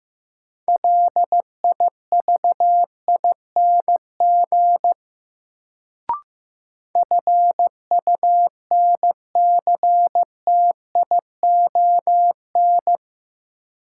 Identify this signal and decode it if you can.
Morse code, heard as a keyed tone, about 15 wpm, 700 Hz. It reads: LIVING FUNCTION
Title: Sentences from Top 1100 Words - New Word Drill Alt - 15wpm